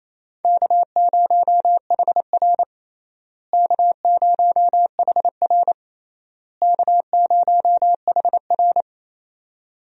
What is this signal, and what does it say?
Morse code, heard as a keyed tone, about 28 wpm, 700 Hz. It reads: K0HR K0HR K0HR
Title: USA Call Signs - 4 Characters - Repeated 3x - 28wpm